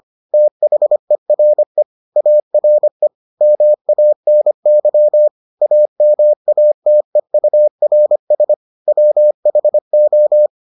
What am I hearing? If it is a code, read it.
Morse code, heard as a keyed tone, about 25 wpm, 600 Hz. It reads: THERE ARE MANY AMATEURS WHO